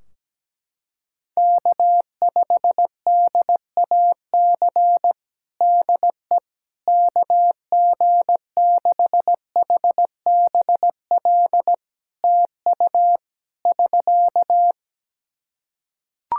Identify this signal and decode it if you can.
Morse code, heard as a keyed tone, about 17 wpm, 700 Hz. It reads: K5DAC DE KG6HBL TU <SK>
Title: K1USN SST QSO Practice 17wpm